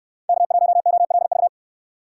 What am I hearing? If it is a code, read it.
Morse code, heard as a keyed tone, about 67 wpm, 700 Hz. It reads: K1OLV